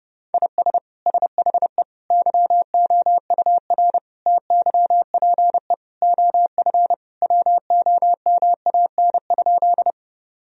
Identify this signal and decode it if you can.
Morse code, heard as a keyed tone, about 30 wpm, 700 Hz. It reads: IS SHE YOUR TYPE OF WOMAN?
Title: Sentences from Top 1000 Words - Review Part 3 - 30wpm